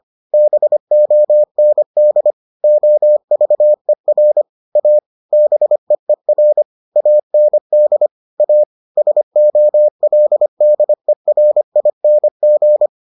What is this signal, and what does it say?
Morse code, heard as a keyed tone, about 25 wpm, 600 Hz. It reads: BOND OVER A BEER AND A SOLDERING